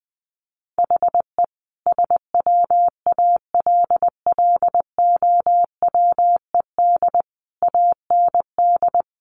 Morse code, heard as a keyed tone, about 20 wpm, 700 Hz. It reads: HE SWALLOWED AND
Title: A Sound Of Thunder 20wpm